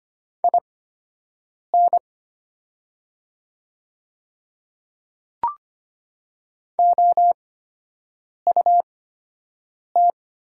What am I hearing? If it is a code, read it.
Morse code, heard as a keyed tone, about 25 wpm, spaced as for 7 wpm, 700 Hz. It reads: IN OUT
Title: 8x Character Spacing - Top 100 Words - 25wpm